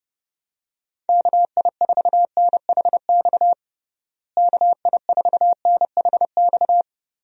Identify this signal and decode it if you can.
Morse code, heard as a keyed tone, about 32 wpm, 700 Hz. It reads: KI4NHX KI4NHX
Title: USA Call Signs - 6 Characters - Repeated 2x - 30wpm